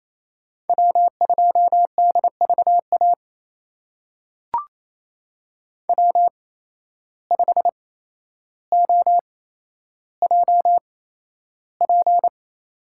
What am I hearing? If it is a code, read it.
Morse code, heard as a keyed tone, about 27 wpm, spaced as for 12 wpm, 700 Hz. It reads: W2DVA W5OJP
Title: USA Call Signs - 5 Characters - 8x Character Spacing - 28wpm